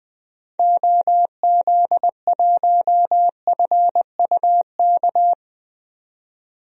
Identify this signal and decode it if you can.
Morse code, heard as a keyed tone, about 20 wpm, 700 Hz. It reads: OZ1FUK
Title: DXCC-100 Call Signs - 6 Characters - 20wpm